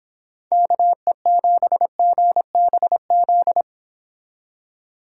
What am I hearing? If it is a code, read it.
Morse code, heard as a keyed tone, about 26 wpm, 700 Hz. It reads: KE7GBZ